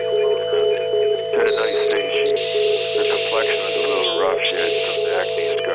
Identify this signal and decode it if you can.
Congestion tone (425 Hz), with speech.